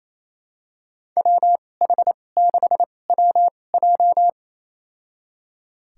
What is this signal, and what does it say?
Morse code, heard as a keyed tone, about 28 wpm, spaced as for 20 wpm, 700 Hz. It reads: WH6WJ
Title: USA Call Signs - 2x Character Spacing - 28wpm